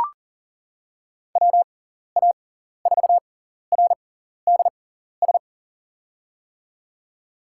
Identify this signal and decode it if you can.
Morse code, heard as a keyed tone, about 40 wpm, spaced as for 14 wpm, 700 Hz. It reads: WA4RDS
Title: USA Call Signs - 6x Character Spacing - 40wpm